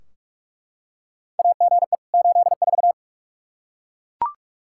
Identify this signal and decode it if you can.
Morse code, heard as a keyed tone, about 43 wpm, 700 Hz. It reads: AGE 84
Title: Practice QSO - Exchange 3 - More Info - 45wpm